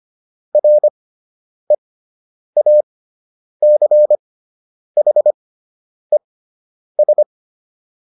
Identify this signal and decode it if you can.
Morse code, heard as a keyed tone, about 25 wpm, spaced as for 9 wpm, 600 Hz. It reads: REACHES